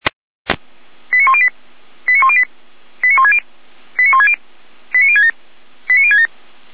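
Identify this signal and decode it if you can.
Five-tone selective call (ZVEI), heard as quick runs of five tones, tones of about 70 ms. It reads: 89108 89108 89160 89160 89067 89067
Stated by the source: Five-tone ZVEI selective call received on SDR, from audio.f85360.feuerwehr-orlamuende.org (Feuerwehr Orlamünde, fire brigade).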